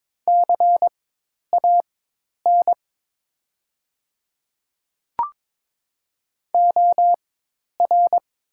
Morse code, heard as a keyed tone, about 22 wpm, spaced as for 10 wpm, 690 Hz. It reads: CAN OR
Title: 4x Character Spacing - Top 200 Words - 22wpm